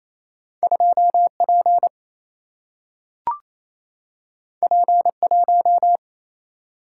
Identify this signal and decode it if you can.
Morse code, heard as a keyed tone, about 28 wpm, 700 Hz. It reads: 2P P1
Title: Single-Letter Single-Number 28wpm